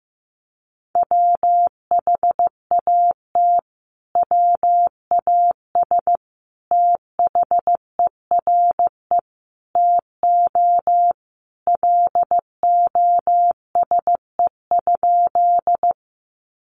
Morse code, heard as a keyed tone, about 15 wpm, 700 Hz. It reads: WHAT WAS THERE TO LOSE?
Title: Sentences from Top 1100 Words 15wpm